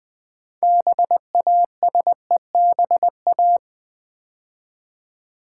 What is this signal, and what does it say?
Morse code, heard as a keyed tone, about 20 wpm, 700 Hz. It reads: BASEBA
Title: Build A Word - 3 Letters At A Time - 8-Letter Words - 20wpm